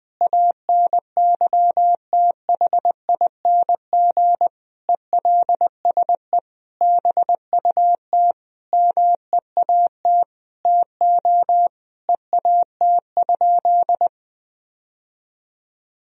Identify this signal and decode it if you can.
Morse code, heard as a keyed tone, about 20 wpm, 700 Hz. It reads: ANYTHING ELSE BUT MEAT TO EAT?
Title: Sentences from Top 1000 Words - Review Part 3 - 20wpm